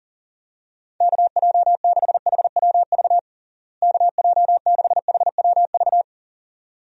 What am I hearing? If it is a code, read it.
Morse code, heard as a keyed tone, about 40 wpm, 700 Hz. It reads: KJ6HWV KJ6HWV
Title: USA Call Signs - 6 Characters - Repeated 2x - 40wpm